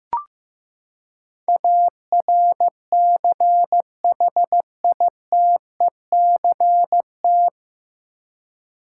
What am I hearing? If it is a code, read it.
Morse code, heard as a keyed tone, about 15 wpm, 700 Hz. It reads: ARCHITECT
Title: Practice QSO - Exchange 3 - More Info - 15wpm